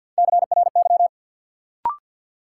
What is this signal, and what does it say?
Morse code, heard as a keyed tone, about 50 wpm, 700 Hz. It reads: CRY
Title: Sentences from Top 500 Words - New Word Drill - 50wpm